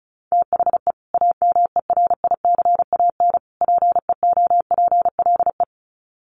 Morse code, heard as a keyed tone, about 34 wpm, 700 Hz. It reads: THE AMERICAN PEOPLE
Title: Sets of 3 Words 35wpm